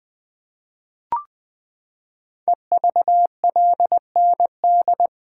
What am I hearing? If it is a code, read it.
Morse code, heard as a keyed tone, about 20 wpm, 700 Hz. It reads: EVLND
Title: Code Groups - 5 Letter:Numbers - 20wpm